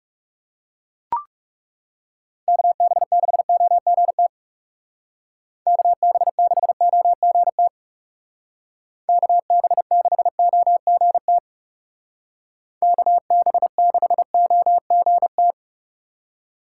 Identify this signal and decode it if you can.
Morse code, heard as a keyed tone, about 36 wpm, 700 Hz. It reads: KB6OGT KB6OGT KB6OGT KB6OGT
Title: Speed Racing - USA 2x3 Call Signs 30wpm